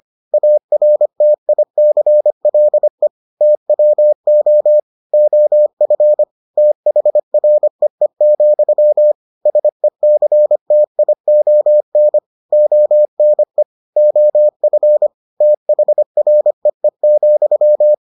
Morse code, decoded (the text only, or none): ARTICLE TWO OF THREE, SECTION ONE OF THREE,